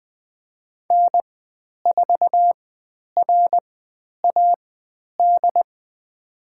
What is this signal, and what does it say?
Morse code, heard as a keyed tone, about 20 wpm, spaced as for 10 wpm, 700 Hz. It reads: N4RAD
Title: 109 Lesson 18g - Letter m - Call Signs 20wpm